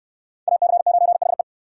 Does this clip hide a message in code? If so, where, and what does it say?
Yes: Morse code from 0.5 to 1.4 s, heard as a keyed tone, about 69 wpm, 690 Hz. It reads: AC9HE